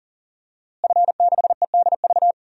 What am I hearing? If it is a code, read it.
Morse code, heard as a keyed tone, about 39 wpm, 700 Hz. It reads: F6EDV
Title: DXCC-100 Call Signs - 5 Characters - Repeated 2x - 40wpm